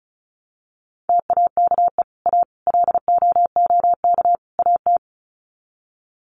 Morse code, heard as a keyed tone, about 34 wpm, 700 Hz. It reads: TAKE A LOOK AT
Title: Sets of 4 Words 35wpm